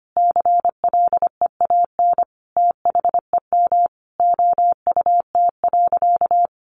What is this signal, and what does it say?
Morse code, heard as a keyed tone, about 25 wpm, 700 Hz. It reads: CLEAN THEM OUT.
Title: Sentences from Top 1500 Words 25wpm